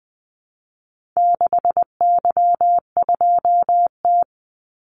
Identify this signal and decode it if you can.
Morse code, heard as a keyed tone, about 20 wpm, 700 Hz. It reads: 6Y2T
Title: Call Signs Repeated 3x - 20wpm